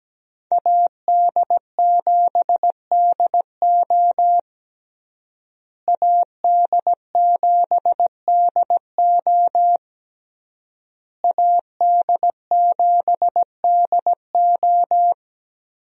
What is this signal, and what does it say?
Morse code, heard as a keyed tone, about 17 wpm, 700 Hz. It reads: AD7DO AD7DO AD7DO
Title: USA Call Signs - 5 Characters - Repeated 3x - 17wpm